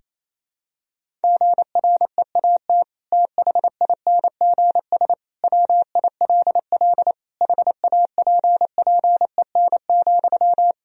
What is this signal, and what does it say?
Morse code, heard as a keyed tone, about 27 wpm, 700 Hz. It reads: GREAT THINGS WILL HAPPEN,
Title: Sentences from Top 1000 Words - Review Part 4 - 28wpm